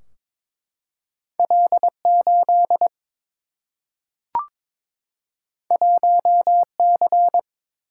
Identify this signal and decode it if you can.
Morse code, heard as a keyed tone, about 22 wpm, 700 Hz. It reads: L 8 1 C
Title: Single-Letter Single-Number 22wpm